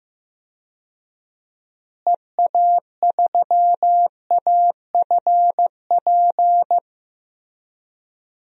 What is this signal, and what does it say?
Morse code, heard as a keyed tone, about 15 wpm, 700 Hz. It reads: EA3AFP
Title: DXCC-100 Call Signs - 6 Characters - Repeated 2x - 15wpm